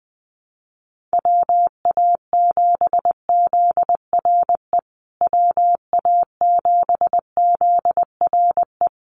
Morse code, heard as a keyed tone, about 20 wpm, 700 Hz. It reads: WA7ZRE WA7ZRE
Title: USA Call Signs Repeated 2x 20wpm